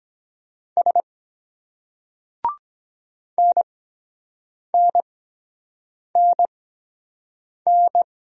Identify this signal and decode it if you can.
Morse code, heard as a keyed tone, about 23 wpm, spaced as for 7 wpm, 700 Hz. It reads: S NNNN